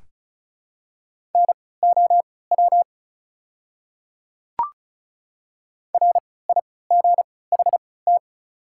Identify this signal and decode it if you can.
Morse code, heard as a keyed tone, about 36 wpm, spaced as for 20 wpm, 690 Hz. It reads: NOW RIGHT
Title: 3x Character Spacing - Top 200 Words - 35wpm